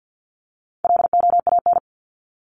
Morse code, heard as a keyed tone, about 50 wpm, 700 Hz. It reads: LOAN
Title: Top 1501-1750 Words 50wpm